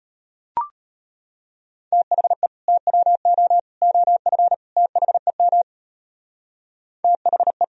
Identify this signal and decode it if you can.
Morse code, heard as a keyed tone, about 30 wpm, 700 Hz. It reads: THE TWO OF THEM THE
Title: Speed-Racing - Sets of 4 Words 25wpm